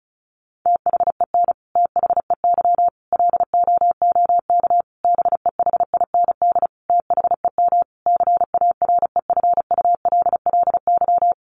Morse code, heard as a keyed tone, about 34 wpm, 700 Hz. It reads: THEN THEY LOOK BEHIND THEM CAREFULLY
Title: Sentences from Top 1300 Words 35wpm